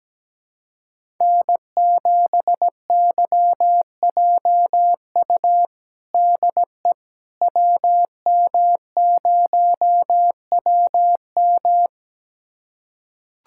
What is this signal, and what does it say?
Morse code, heard as a keyed tone, about 17 wpm, 700 Hz. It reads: N7YJU DE WM0WM
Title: USA Callsign DE Callsign - 5 Characters - 17wpm